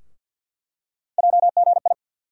Morse code, heard as a keyed tone, about 48 wpm, 700 Hz. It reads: JGI
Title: USA Call Sign Postfixes - 3 Letters 50wpm